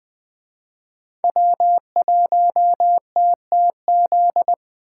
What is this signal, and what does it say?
Morse code, heard as a keyed tone, about 20 wpm, 700 Hz. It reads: W1TTZ
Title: Speed Racing - USA 1x3 Call Signs 17wpm